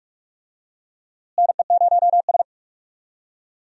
Morse code, heard as a keyed tone, about 46 wpm, 700 Hz. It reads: NE0S
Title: USA Call Signs - 4 Characters - Repeated 3x - 45wpm